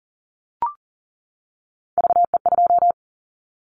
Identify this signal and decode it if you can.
Morse code, heard as a keyed tone, about 42 wpm, 690 Hz. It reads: VE2
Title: Callsign Prefixes for Top 100 DXCC - 40wpm